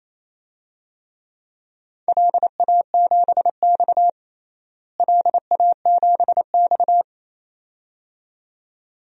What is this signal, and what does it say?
Morse code, heard as a keyed tone, about 28 wpm, 700 Hz. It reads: LA7X LA7X
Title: DXCC-100 Call Signs - 4 Characters - Repeated 2x - 28wpm